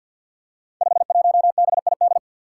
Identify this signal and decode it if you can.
Morse code, heard as a keyed tone, about 47 wpm, 700 Hz. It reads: 51BID
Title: Five Challenging Letter-Number Combinations 50wpm